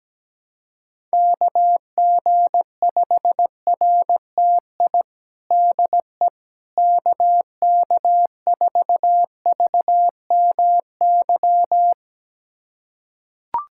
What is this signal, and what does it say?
Morse code, heard as a keyed tone, about 17 wpm, 700 Hz. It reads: KG5RTI DE KK4VMY
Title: USA Callsign DE Callsign 17wpm